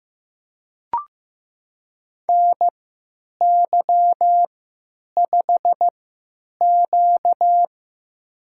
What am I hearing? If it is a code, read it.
Morse code, heard as a keyed tone, about 15 wpm, spaced as for 9 wpm, 700 Hz. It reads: NY5Q
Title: USA Call Signs - 4 Characters - 3x Character Spacing - 15wpm